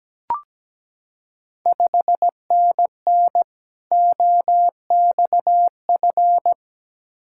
Morse code, heard as a keyed tone, about 17 wpm, 700 Hz. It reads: 5NN OXF
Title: Ontario QSO Party Exchange 17wpm